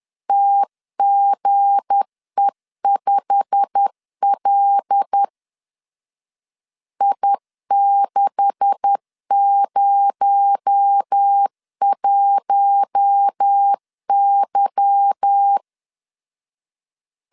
Morse code, heard as a keyed tone, about 10 wpm, 800 Hz. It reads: TGE5L I601Y
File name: LEZIONI / 10 SIGNS / 10B INT 10wpm 2844